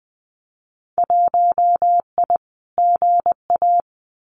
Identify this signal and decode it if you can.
Morse code, heard as a keyed tone, about 20 wpm, 700 Hz. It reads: 1I GA
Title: Winter Field Day Exchanges Encore 20wpm